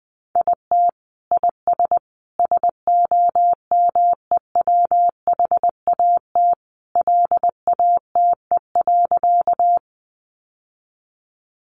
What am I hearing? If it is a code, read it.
Morse code, heard as a keyed tone, about 19 wpm, 700 Hz. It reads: IT IS SOMEWHAT LATE.
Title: Sentences from Top 1300 Words 20wpm